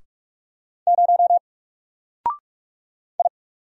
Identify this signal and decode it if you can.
Morse code, heard as a keyed tone, about 44 wpm, 700 Hz. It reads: 0 I
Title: Single Character (Letter:Number:Punctuation:Prosign) 45wpm